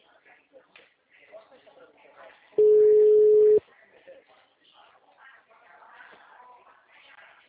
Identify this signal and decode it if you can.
Ringback tone (Europe, 425 Hz), with speech.
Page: Atibaia 3